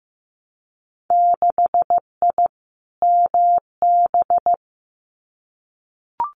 Morse code, heard as a keyed tone, about 15 wpm, 700 Hz. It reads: 6I MB
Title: Winter Field Day Exchanges Encore 15wpm